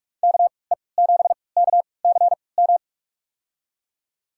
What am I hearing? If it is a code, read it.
Morse code, heard as a keyed tone, about 47 wpm, spaced as for 26 wpm, 700 Hz. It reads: KE7KCM
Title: USA Call Signs - 6 Characters - 3x Character Spacing - 45wpm